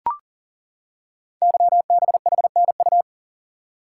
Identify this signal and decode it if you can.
Morse code, heard as a keyed tone, about 38 wpm, 690 Hz. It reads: YBHNU
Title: Code Groups - 5 Letter:Numbers - 40wpm